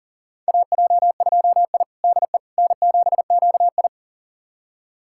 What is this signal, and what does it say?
Morse code, heard as a keyed tone, about 40 wpm, 700 Hz. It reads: AJ2I DE N7QI
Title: USA Callsign DE Callsign - 4 Characters - 40wpm